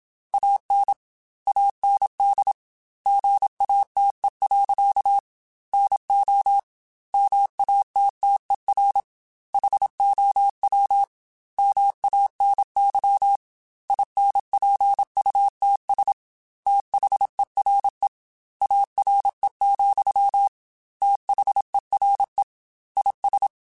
Morse code, heard as a keyed tone, about 27 wpm, 800 Hz. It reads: AN AND GATE. NO MATTER HOW MANY INPUTS THERE ARE, THERE IS
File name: Technical Subjects / Technical Subjects-mp3s / AND Gate @25